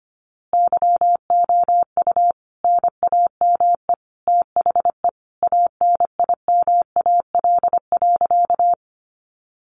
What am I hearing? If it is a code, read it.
Morse code, heard as a keyed tone, about 25 wpm, 700 Hz. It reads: YOU NAME THE ANIMAL.
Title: A Sound Of Thunder 25wpm